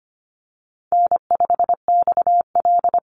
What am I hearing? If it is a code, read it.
Morse code, heard as a keyed tone, about 25 wpm, 700 Hz. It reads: N5XL
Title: NA SOTA Chaser Callsigns 25wpm